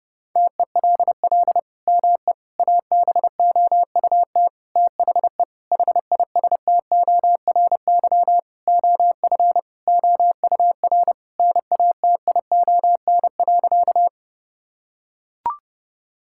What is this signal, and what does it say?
Morse code, heard as a keyed tone, about 30 wpm, 700 Hz. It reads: TELL ME ABOUT THE HISTORY OF OUR NATION.
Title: Sentences from Top 1000 Words - Review Part 6 - 30wpm